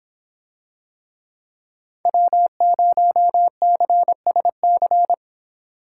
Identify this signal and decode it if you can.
Morse code, heard as a keyed tone, about 23 wpm, 700 Hz. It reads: W0CSC